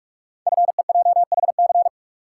Morse code, heard as a keyed tone, about 44 wpm, 700 Hz. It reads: FEJHC
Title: Code Groups - 5 Letters - 45wpm